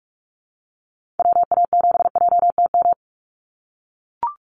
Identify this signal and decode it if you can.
Morse code, heard as a keyed tone, about 44 wpm, 700 Hz. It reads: WA7JTM
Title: NA SOTA Activator Callsigns 45wpm